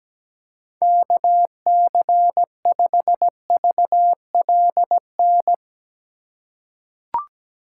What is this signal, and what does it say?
Morse code, heard as a keyed tone, about 17 wpm, 700 Hz. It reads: KC5VLN
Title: USA Call Signs - 6 Characters - 3x Character Spacing - 17wpm